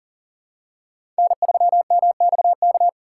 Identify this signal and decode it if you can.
Morse code, heard as a keyed tone, about 40 wpm, 700 Hz. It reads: N3MXK
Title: USA Call Signs - 5 Characters - 4x Character Spacing - 40wpm